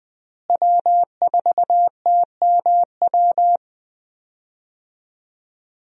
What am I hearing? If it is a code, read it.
Morse code, heard as a keyed tone, about 20 wpm, 700 Hz. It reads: W4TMW
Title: USA Call Signs - 5 Characters - Repeated 2x - 20wpm